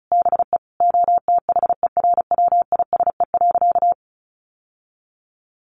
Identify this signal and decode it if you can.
Morse code, heard as a keyed tone, about 34 wpm, 700 Hz. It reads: BE OTHERWISE.
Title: Sentences from Top 1200 Words 35wpm